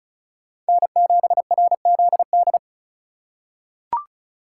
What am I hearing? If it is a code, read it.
Morse code, heard as a keyed tone, about 34 wpm, 700 Hz. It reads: N7RZD
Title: USA Call Signs - 5 Characters - 4x Character Spacing - 35wpm